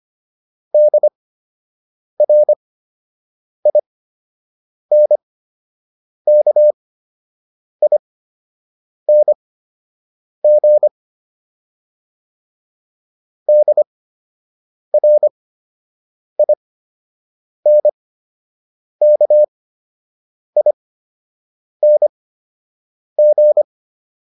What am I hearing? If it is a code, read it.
Morse code, heard as a keyed tone, about 25 wpm, spaced as for 7 wpm, 600 Hz. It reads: DRINKING DRINKING